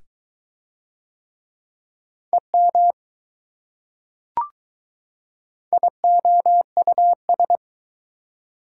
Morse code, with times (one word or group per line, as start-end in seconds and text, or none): 2.3-2.9 EM
5.7-7.6 IOUS